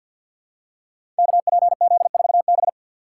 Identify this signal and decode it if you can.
Morse code, heard as a keyed tone, about 48 wpm, 700 Hz. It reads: KPZ4B
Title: Code Groups - 5 Letter:Numbers - 50wpm